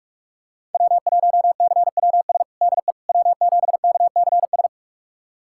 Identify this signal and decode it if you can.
Morse code, heard as a keyed tone, about 45 wpm, 700 Hz. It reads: W1CWS DE W7KCS
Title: USA Callsign DE Callsign - 5 Characters - 45wpm